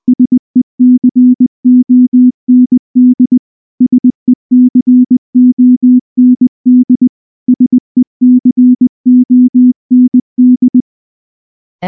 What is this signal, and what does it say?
Morse code, heard as a keyed tone, about 19 wpm, 250 Hz. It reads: SECOND SECOND SECOND
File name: morse / morse-training-20WPM-5